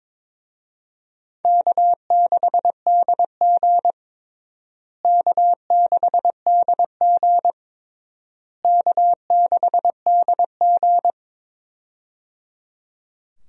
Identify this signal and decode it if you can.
Morse code, heard as a keyed tone, about 22 wpm, 700 Hz. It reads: K6DG K6DG K6DG
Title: USA Call Signs - 4 Characters - Repeated 3x - 22wpm